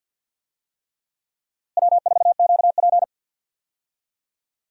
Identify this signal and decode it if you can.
Morse code, heard as a keyed tone, about 51 wpm, 700 Hz. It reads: W4QP
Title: USA Call Signs - 4 Characters - Repeated 3x - 50wpm